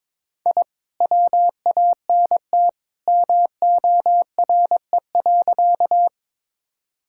Morse code, heard as a keyed tone, about 22 wpm, 700 Hz. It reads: I WANT MORE.
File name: practice / 22wpm / Sentences with 2-Word Phrases - 22wpm